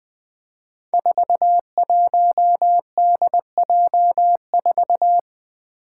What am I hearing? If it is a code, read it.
Morse code, heard as a keyed tone, about 20 wpm, 700 Hz. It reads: 41DJ4
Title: Five Challenging Letter-Number Combinations 20wpm